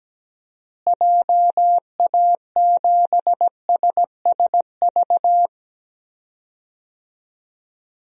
Morse code, heard as a keyed tone, about 17 wpm, 700 Hz. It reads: JA7SSV
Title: DXCC-100 Call Signs - 6 Characters - Repeated 3x - 17wpm